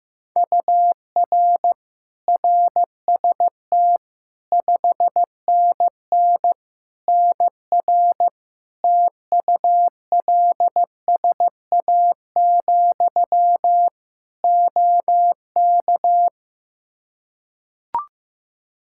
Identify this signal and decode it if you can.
Morse code, heard as a keyed tone, about 15 wpm, 700 Hz. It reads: UR RST 5NN NR TULSA, OK
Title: Practice QSO - Exchange 1 - Beginning - 15wpm